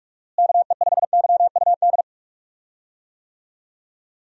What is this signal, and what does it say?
Morse code, heard as a keyed tone, about 46 wpm, 700 Hz. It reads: KE5YUD
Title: USA Call Signs - 6 Characters - Repeated 2x - 45wpm